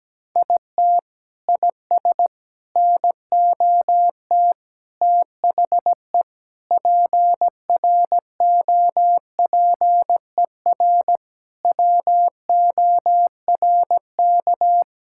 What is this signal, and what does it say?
Morse code, heard as a keyed tone, about 17 wpm, 700 Hz. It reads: IT IS NOT THE PROPER WORK
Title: Sentences from Top 1000 Words - Review Part 5 - 17wpm